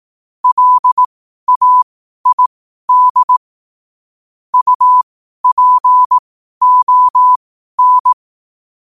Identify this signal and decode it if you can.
Morse code, heard as a keyed tone, about 18 wpm, spaced as for 12 wpm, 1000 Hz. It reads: LAID UPON
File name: 12wpm